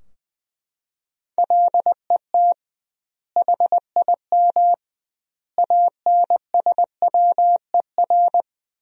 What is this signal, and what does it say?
Morse code, heard as a keyed tone, about 21 wpm, 700 Hz. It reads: LET HIM ANSWER
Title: 2x Word-Spacing - Sentences from Top 300 Words - 20wpm